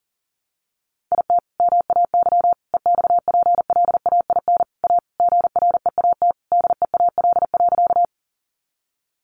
Morse code, heard as a keyed tone, about 40 wpm, 700 Hz. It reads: IT MAY EXPLAIN A GREAT DEAL.
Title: Sentences from Top 1100 Words 40wpm